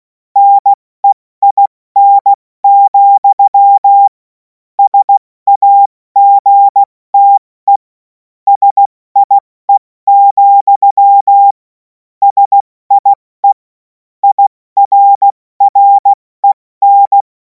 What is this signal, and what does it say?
Morse code, heard as a keyed tone, about 16 wpm, 800 Hz. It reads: NEIN, SAGTE SIE, SIE IRREN
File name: Morsekurs / Witze / Witz 07. April 2026 70 F80